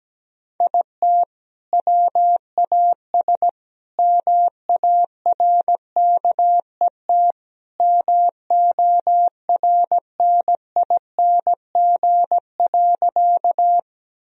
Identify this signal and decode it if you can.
Morse code, heard as a keyed tone, about 17 wpm, 700 Hz. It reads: IT WAS MARKET MORNING.
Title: Sentences from Top 1000 Words - Review Part 3 - 17wpm